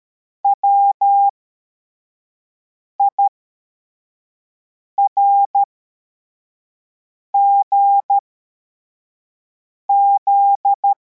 Morse code, heard as a keyed tone, about 13 wpm, spaced as for 4 wpm, 800 Hz. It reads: WIRGZ
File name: formation / cw / Text64